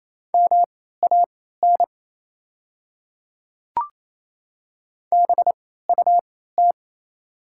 Morse code, heard as a keyed tone, about 28 wpm, spaced as for 16 wpm, 700 Hz. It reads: MAN BUT
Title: 3x Character Spacing - Top 300 Words - 28wpm